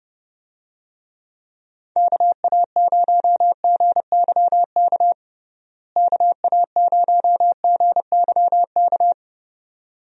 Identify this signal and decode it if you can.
Morse code, heard as a keyed tone, about 30 wpm, 700 Hz. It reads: KA0GYK KA0GYK
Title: USA Call Signs - 6 Characters - Repeated 2x - 30wpm